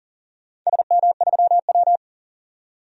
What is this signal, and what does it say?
Morse code, heard as a keyed tone, about 38 wpm, 700 Hz. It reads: SM3W
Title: DXCC-100 Call Signs - 4 Characters - 40wpm